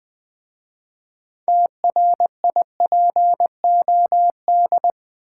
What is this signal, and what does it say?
Morse code, heard as a keyed tone, about 20 wpm, 700 Hz. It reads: TRIPOD
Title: Easy Words with Hints - 20wpm